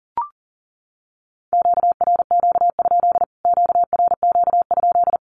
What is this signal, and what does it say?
Morse code, heard as a keyed tone, about 40 wpm, 700 Hz. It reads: QRQ? QRQ?
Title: Q-Codes Repeated 3x 40wpm